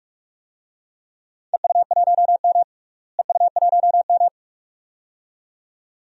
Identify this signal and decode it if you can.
Morse code, heard as a keyed tone, about 42 wpm, 700 Hz. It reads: EU1M EU1M
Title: DXCC-100 Call Signs - 4 Characters - Repeated 2x - 45wpm